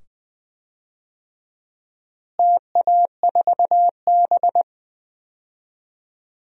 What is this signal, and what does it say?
Morse code, heard as a keyed tone, about 20 wpm, 700 Hz. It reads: TA4B
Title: DXCC-100 Call Signs - 4 Characters - Repeated 2x - 20wpm